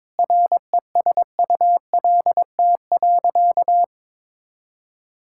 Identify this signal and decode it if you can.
Morse code, heard as a keyed tone, about 22 wpm, 700 Hz. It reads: RESULT.
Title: Sentences from Top 1000 Words - Review Part 5 - 22wpm